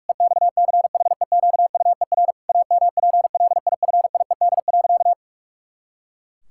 Morse code, heard as a keyed tone, about 46 wpm, 700 Hz. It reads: EXCHEQUER AMPLIFIED.
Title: Pangram Challenge v2 - 45wpm